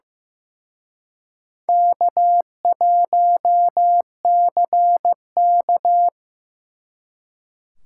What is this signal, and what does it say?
Morse code, heard as a keyed tone, about 18 wpm, 700 Hz. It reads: K1CK
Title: Speed Racing - USA 1x2 Call Signs 15wpm